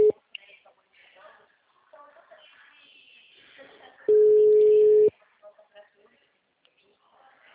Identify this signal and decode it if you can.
Ringback tone (Europe, 425 Hz), with speech.